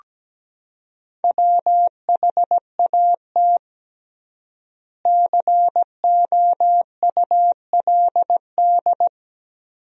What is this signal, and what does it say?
Morse code, heard as a keyed tone, about 17 wpm, 700 Hz. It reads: WHAT COULD
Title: 3x Word-Spacing - Sentences from Top 100 Words - 17wpm